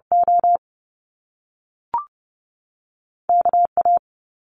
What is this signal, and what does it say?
Morse code, heard as a keyed tone, about 30 wpm, spaced as for 3 wpm, 700 Hz. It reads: OK A
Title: Top Two Letter Combinations in Words 30wpm